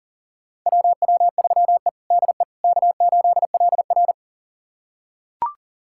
Morse code, heard as a keyed tone, about 37 wpm, 700 Hz. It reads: WW3E DE K8LR
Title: USA Callsign DE Callsign - 4 Characters - 40wpm